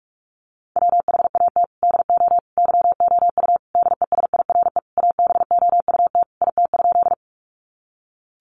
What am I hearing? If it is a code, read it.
Morse code, heard as a keyed tone, about 44 wpm, 700 Hz. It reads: WHAT DO YOU DESIRE ABOUT IT?
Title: Sentences from Top 1100 Words 45wpm